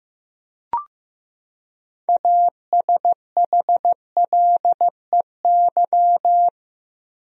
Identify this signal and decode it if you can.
Morse code, heard as a keyed tone, about 15 wpm, 700 Hz. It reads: ASHLEY
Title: USA Common Names - 15wpm